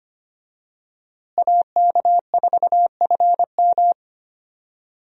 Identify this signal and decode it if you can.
Morse code, heard as a keyed tone, about 25 wpm, 700 Hz. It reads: AK4FM
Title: USA Call Signs - 5 Characters - Repeated 2x - 25wpm